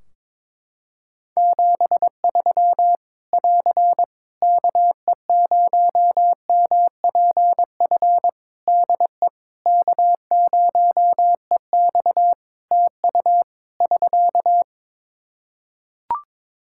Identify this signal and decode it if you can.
Morse code, heard as a keyed tone, about 22 wpm, 700 Hz. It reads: 73 <AR> KE0MPF DE K0EX TU <SK>
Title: Practice QSO - Exchange 4 - Ending - 22wpm